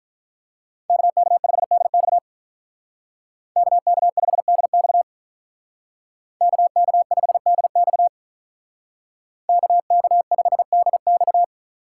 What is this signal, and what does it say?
Morse code, heard as a keyed tone, about 43 wpm, 700 Hz. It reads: KK5DX KK5DX KK5DX KK5DX
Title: Speed Racing - USA 2x2 Call Signs 35wpm